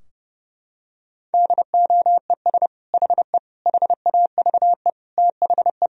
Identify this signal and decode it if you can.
Morse code, heard as a keyed tone, about 30 wpm, 700 Hz. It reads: DOES HE HAVE THE
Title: Sentences from Top 1000 Words - Review Part 3 - 30wpm